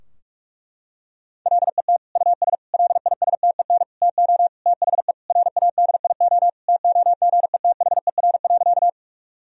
Morse code, heard as a keyed tone, about 41 wpm, 700 Hz. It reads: LET US LISTEN TO THE RADIO TOGETHER.
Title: Sentences from Top 1000 Words - Review Part 3 - 45wpm